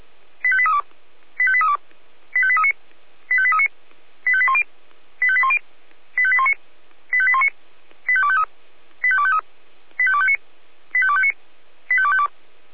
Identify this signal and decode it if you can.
Five-tone selective call (ZVEI), heard as quick runs of five tones, tones of about 70 ms. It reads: 85832 85832 85839 85839 85810 85810 85819 85819 85253 85253 85259 85259 85252
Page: SDR/ZVEI